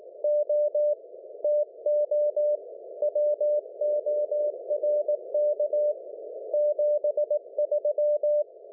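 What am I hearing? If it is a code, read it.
Morse code, heard as a keyed tone, about 18 wpm, 590 Hz. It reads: O TO WORK 73
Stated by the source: CW from G4GTW/M on 1942 kHz in the 160 m band.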